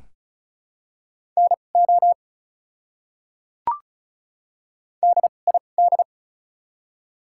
Morse code, heard as a keyed tone, about 36 wpm, spaced as for 25 wpm, 700 Hz. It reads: NO DID
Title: 2x Character Spacing - Top 100 Words - 35wpm